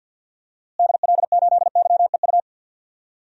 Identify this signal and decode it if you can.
Morse code, heard as a keyed tone, about 52 wpm, 700 Hz. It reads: DL8YEU
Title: DXCC-100 Call Signs - 6 Characters - Repeated 2x - 50wpm